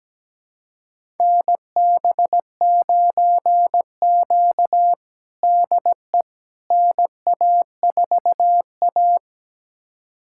Morse code, heard as a keyed tone, about 17 wpm, 700 Hz. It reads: NB9Q DE NA4A
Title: USA Callsign DE Callsign - 4 Characters - 17wpm